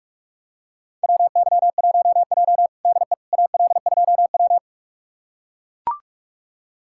Morse code, heard as a keyed tone, about 44 wpm, 700 Hz. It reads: WY1J DE AL2W
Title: USA Callsign DE Callsign - 4 Characters - 45wpm